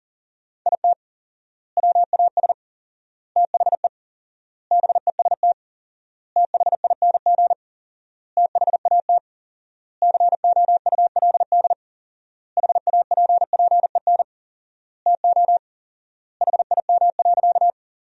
Morse code, heard as a keyed tone, about 40 wpm, 700 Hz. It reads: IT WAS THE BEST THING THAT COULD HAPPEN TO HIM.
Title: 4x Word-Spacing - Sentences from Top 400 Words - 40wpm